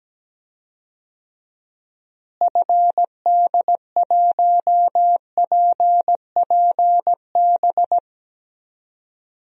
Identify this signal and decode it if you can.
Morse code, heard as a keyed tone, about 17 wpm, 700 Hz. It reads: FD1PPB
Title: DXCC-100 Call Signs - 6 Characters - Repeated 3x - 17wpm